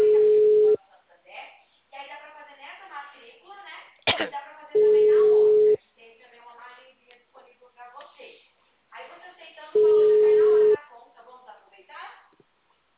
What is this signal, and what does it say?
Ringback tone (Europe, 425 Hz), with speech.
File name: out- -2004- - - .9395